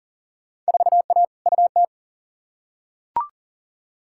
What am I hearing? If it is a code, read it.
Morse code, heard as a keyed tone, about 40 wpm, 700 Hz. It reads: 4A UT
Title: Field Day Exchanges - 40wpm